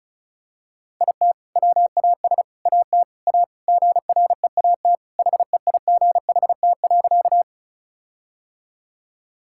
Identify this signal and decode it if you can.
Morse code, heard as a keyed tone, about 36 wpm, 700 Hz. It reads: IT WAS AT A GREAT HEIGHT.
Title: Sentences from Top 1600 Words 35wpm